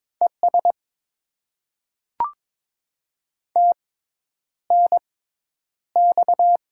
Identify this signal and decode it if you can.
Morse code, heard as a keyed tone, about 22 wpm, spaced as for 10 wpm, 700 Hz. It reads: ES TNX
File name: Top 20 QSO Elements - 6x Character Spacing - 22wpm